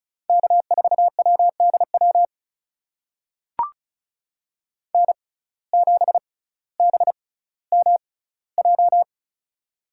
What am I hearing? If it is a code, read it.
Morse code, heard as a keyed tone, about 34 wpm, spaced as for 18 wpm, 700 Hz. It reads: K4WDW N7BMJ
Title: USA Call Signs - 5 Characters - 6x Character Spacing - 35wpm